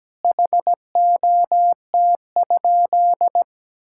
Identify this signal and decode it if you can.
Morse code, heard as a keyed tone, about 17 wpm, 700 Hz. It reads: HOT?
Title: Sentences from Top 1000 Words - Review Part 6 - 17wpm